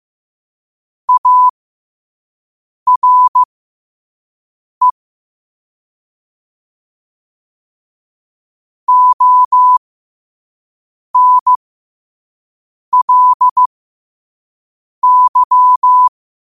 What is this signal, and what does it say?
Morse code, heard as a keyed tone, about 15 wpm, spaced as for 5 wpm, 1000 Hz. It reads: ARE ONLY
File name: qotdfiles / 5wpm